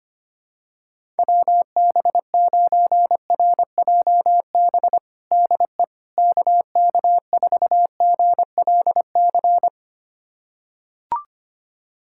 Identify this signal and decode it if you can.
Morse code, heard as a keyed tone, about 25 wpm, 700 Hz. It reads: WB9RJB DE KK4GLC
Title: USA Callsign DE Callsign 25wpm